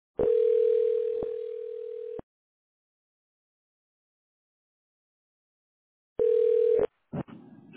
Ringback tone (North America), 440 + 480 Hz.